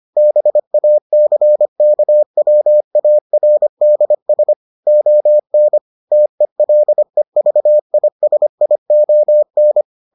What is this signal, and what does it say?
Morse code, heard as a keyed tone, about 25 wpm, 600 Hz. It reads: BACKWARDS ON TELEVISION